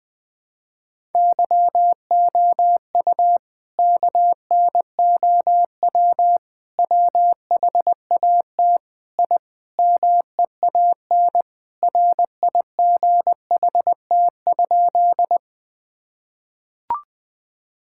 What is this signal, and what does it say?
Morse code, heard as a keyed tone, about 20 wpm, 700 Hz. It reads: YOU KNOW WHAT I MEAN RIGHT?
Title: Sentences with 5-Word Phrases - Spoken Beginning - 20wpm